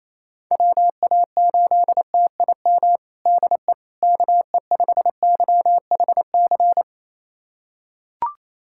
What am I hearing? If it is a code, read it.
Morse code, heard as a keyed tone, about 28 wpm, 700 Hz. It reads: WA8TIM DE KE5YHC
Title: USA Callsign DE Callsign 28wpm